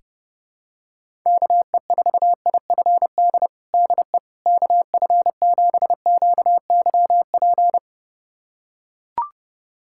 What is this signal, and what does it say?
Morse code, heard as a keyed tone, about 30 wpm, 700 Hz. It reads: KE4IFD DE KF7QYP
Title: USA Callsign DE Callsign - 6 Characters - 30wpm